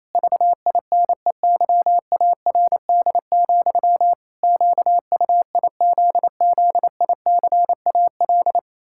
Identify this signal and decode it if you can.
Morse code, heard as a keyed tone, about 28 wpm, 700 Hz. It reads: VINEYARD, QUIZZICAL
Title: Pangram Challenge v2 - 28wpm